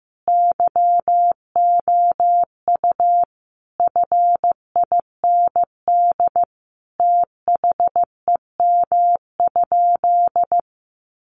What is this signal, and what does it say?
Morse code, heard as a keyed tone, about 15 wpm, 700 Hz. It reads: YOU FIND THEM?
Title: Sentences from Top 200 Words 15wpm